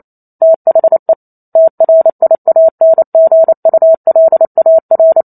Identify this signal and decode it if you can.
Morse code, heard as a keyed tone, about 28 wpm, 650 Hz. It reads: THE TRIANGULAR